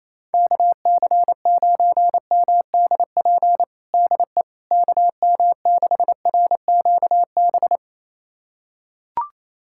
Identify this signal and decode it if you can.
Morse code, heard as a keyed tone, about 28 wpm, 700 Hz. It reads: KC9MDP DE KM6RQB
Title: USA Callsign DE Callsign - 6 Characters - 28wpm